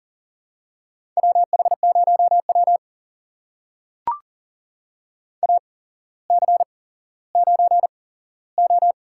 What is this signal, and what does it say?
Morse code, heard as a keyed tone, about 40 wpm, spaced as for 17 wpm, 700 Hz. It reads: WH0W AC9O
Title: USA Call Signs - 4 Characters - 8x Character Spacing - 40wpm